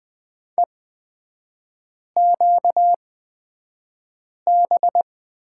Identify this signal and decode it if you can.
Morse code, heard as a keyed tone, about 20 wpm, spaced as for 5 wpm, 700 Hz. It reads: EQB
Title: 236 Lesson 38e - Letter q - Three Characters 20wpm